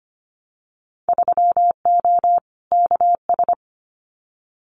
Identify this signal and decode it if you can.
Morse code, heard as a keyed tone, about 25 wpm, 700 Hz. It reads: 3O KS
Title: Winter Field Day Exchanges Encore 25wpm